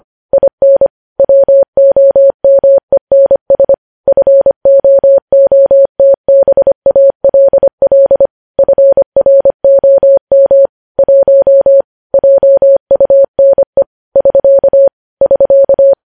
Morse code, heard as a keyed tone, about 25 wpm, 550 Hz. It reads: IN WOMENS FOOTBALL FROM 1 JUNE <SK> <SK>